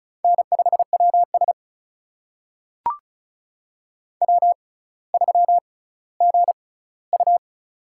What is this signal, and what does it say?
Morse code, heard as a keyed tone, about 36 wpm, spaced as for 18 wpm, 700 Hz. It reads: N5WS W3GU
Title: USA Call Signs - 4 Characters - 6x Character Spacing - 35wpm